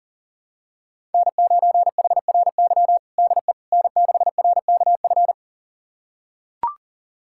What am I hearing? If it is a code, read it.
Morse code, heard as a keyed tone, about 40 wpm, 700 Hz. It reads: N9HRY DE N6RKF
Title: USA Callsign DE Callsign - 5 Characters - 40wpm